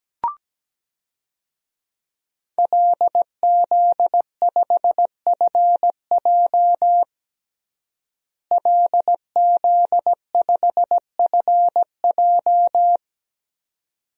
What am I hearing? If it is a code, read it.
Morse code, heard as a keyed tone, about 17 wpm, 700 Hz. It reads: LZ5FJ LZ5FJ
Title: DXCC-100 Call Signs - 5 Characters - Repeated 2x - 17wpm